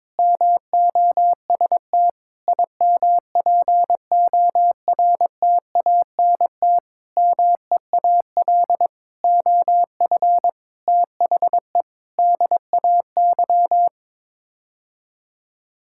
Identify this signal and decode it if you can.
Morse code, heard as a keyed tone, about 22 wpm, 700 Hz. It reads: MOST IMPORTANT MEAL OF THE DAY
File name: Sets of 9 Words - 22wpm